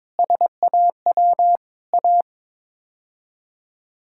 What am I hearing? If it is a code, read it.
Morse code, heard as a keyed tone, about 22 wpm, 700 Hz. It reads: SAW A
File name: practice / 22wpm / Sets of 8 Words - 22wpm